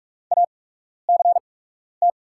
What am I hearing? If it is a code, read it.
Morse code, heard as a keyed tone, about 44 wpm, spaced as for 12 wpm, 690 Hz. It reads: ACT
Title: 8x Character Spacing - Top 300 Words - 45wpm